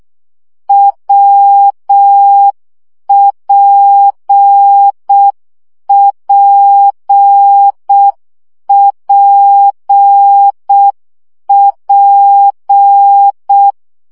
Morse code, heard as a keyed tone, about 6 wpm, 800 Hz. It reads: WPPPP
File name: CW gyakorlas w p j 5 baud